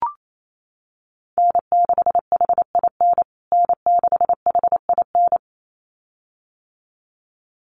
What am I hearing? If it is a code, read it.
Morse code, heard as a keyed tone, about 28 wpm, 700 Hz. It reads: N6HIN N6HIN
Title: USA Call Signs Repeated 2x 28wpm